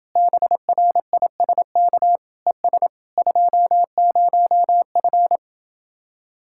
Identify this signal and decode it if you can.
Morse code, heard as a keyed tone, about 31 wpm, 700 Hz. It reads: BRISK ES 20F